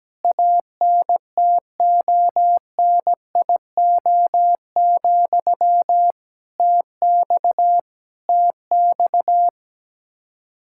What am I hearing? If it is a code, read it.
Morse code, heard as a keyed tone, about 17 wpm, 700 Hz. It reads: ANTONIO, TX TX
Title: K1USN SST QSO Practice 17wpm